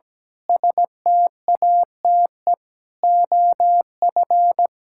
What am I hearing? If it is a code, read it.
Morse code, heard as a keyed tone, about 17 wpm, 700 Hz. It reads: STATE OF
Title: Pangram Challenge v2 - 17wpm